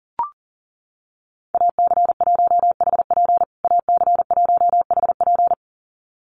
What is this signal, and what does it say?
Morse code, heard as a keyed tone, about 40 wpm, 700 Hz. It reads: AC1HP AC1HP
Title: USA Call Signs Repeated 2x 40wpm